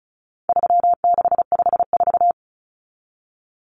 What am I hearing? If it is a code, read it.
Morse code, heard as a keyed tone, about 34 wpm, 700 Hz. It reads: 3654
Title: Common Cut Numbers - 4-Digit 35wpm